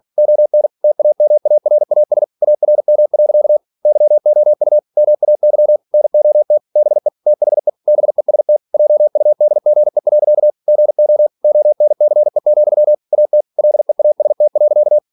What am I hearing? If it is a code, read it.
Morse code, heard as a keyed tone, about 46 wpm, 600 Hz. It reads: ON TAMARAS ARM. YOU MAY NOT BE THE BEST JUDGE. GO ONCE, AT LEAST.